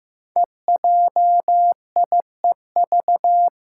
Morse code, heard as a keyed tone, about 15 wpm, 700 Hz. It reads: EJIEV
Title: Five Challenging Letter-Number Combinations 15wpm